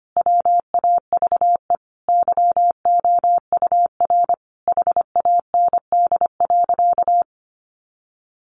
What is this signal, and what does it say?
Morse code, heard as a keyed tone, about 25 wpm, 700 Hz. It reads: WAVE YOUR HAND.
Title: Sentences from Top 1400 Words 25wpm